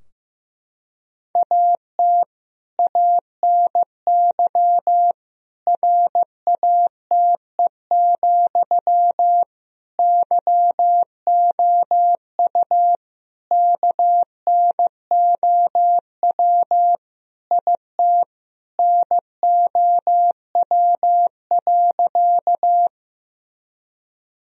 Morse code, heard as a keyed tone, about 15 wpm, 700 Hz. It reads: AT ANY RATE, YOU KNOW IT NOW.
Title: Sentences from Top 1000 Words - Review Part 4 - 15wpm